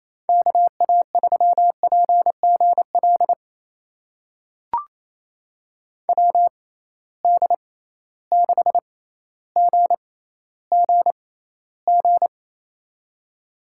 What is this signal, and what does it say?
Morse code, heard as a keyed tone, about 28 wpm, spaced as for 14 wpm, 700 Hz. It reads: KA3PGL WD6GGG
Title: USA Call Signs - 6 Characters - 6x Character Spacing - 28wpm